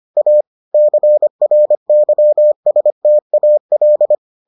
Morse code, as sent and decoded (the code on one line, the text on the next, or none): .- / -.-. .-. -.-- ... - .- .-..
A CRYSTAL